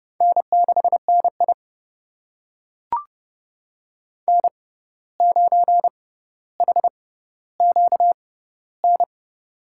Morse code, heard as a keyed tone, about 30 wpm, spaced as for 14 wpm, 700 Hz. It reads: N6NI N9HQN
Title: USA Call Signs - 6x Character Spacing - 30wpm